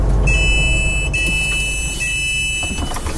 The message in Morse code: none